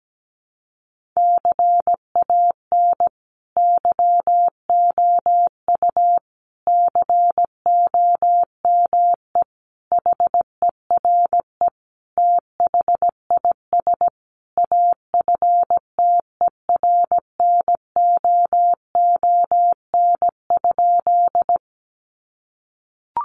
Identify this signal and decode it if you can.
Morse code, heard as a keyed tone, about 17 wpm, 700 Hz. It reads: CAN YOU COME HERE THIS AFTERNOON?
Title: Sentences from Top 1500 Words 17wpm